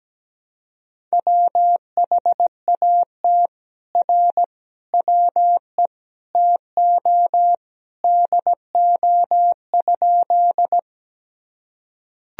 Morse code, heard as a keyed tone, about 17 wpm, 700 Hz. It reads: WHAT R WE TO DO?
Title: Sentences from Top 100 Words and Abbreviations 17wpm